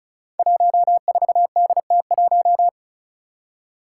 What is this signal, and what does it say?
Morse code, heard as a keyed tone, about 36 wpm, 700 Hz. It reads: 14DT1
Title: Five Challenging Letter-Number Combinations 35wpm